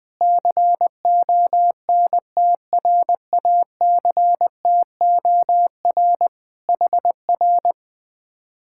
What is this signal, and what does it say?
Morse code, heard as a keyed tone, about 19 wpm, 700 Hz. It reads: CONTRACTOR HR
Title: Speed-Racing Common QSO Phrases - 20wpm